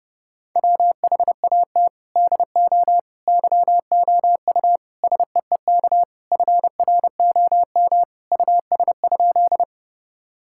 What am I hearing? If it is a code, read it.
Morse code, heard as a keyed tone, about 32 wpm, 700 Hz. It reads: WHAT DO YOU SEEK FROM US?
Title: Sentences from Top 1600 Words 30wpm